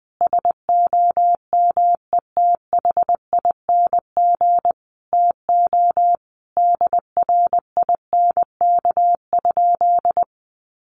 Morse code, heard as a keyed tone, about 20 wpm, 700 Hz. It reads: SOMETHING TO DRINK?
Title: Sentences from Top 1500 Words 20wpm